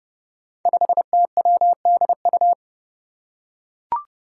Morse code, heard as a keyed tone, about 30 wpm, 700 Hz. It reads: 5TWDU
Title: Five Challenging Letter-Number Combinations 30wpm